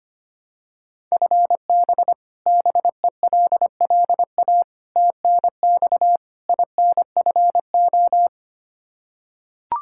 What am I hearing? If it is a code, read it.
Morse code, heard as a keyed tone, about 25 wpm, 700 Hz. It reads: FB BELLA TNX INFO
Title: Practice QSO - Exchange 3 - More Info - 25wpm